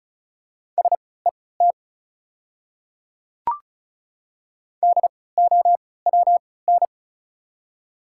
Morse code, heard as a keyed tone, about 36 wpm, spaced as for 20 wpm, 700 Hz. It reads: SET DOWN
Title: 3x Character Spacing - Top 200 Words - 35wpm